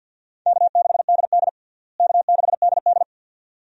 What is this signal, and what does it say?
Morse code, heard as a keyed tone, about 52 wpm, 700 Hz. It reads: K6DD K6DD
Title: USA Call Signs - 4 Characters - Repeated 2x - 50wpm